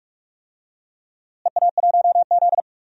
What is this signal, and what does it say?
Morse code, heard as a keyed tone, about 44 wpm, 700 Hz. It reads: EA1Z
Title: DXCC-100 Call Signs - 4 Characters - Repeated 3x - 45wpm